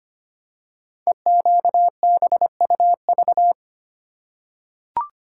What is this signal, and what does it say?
Morse code, heard as a keyed tone, about 25 wpm, 700 Hz. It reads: EQBUV
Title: Code Groups - 5 Letters - 25wpm